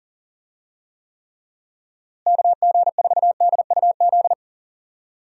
Morse code, heard as a keyed tone, about 37 wpm, 700 Hz. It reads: KG4DUZ
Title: USA Call Signs - 6 Characters - Repeated 3x - 40wpm